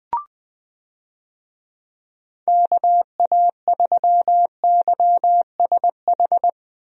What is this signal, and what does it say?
Morse code, heard as a keyed tone, about 20 wpm, 700 Hz. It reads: KA3YSH
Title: USA Call Signs - 6 Characters - 20wpm